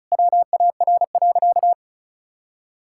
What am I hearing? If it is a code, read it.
Morse code, heard as a keyed tone, about 34 wpm, 700 Hz. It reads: WAR.
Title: Sentences from Top 1000 Words - Review Part 6 - 35wpm